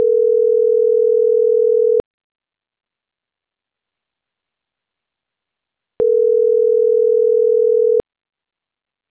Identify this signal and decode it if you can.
Ringback tone (North America), 440 + 480 Hz.